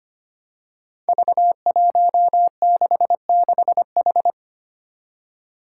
Morse code, heard as a keyed tone, about 25 wpm, 700 Hz. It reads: V166H
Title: Five Challenging Letter-Number Combinations 25wpm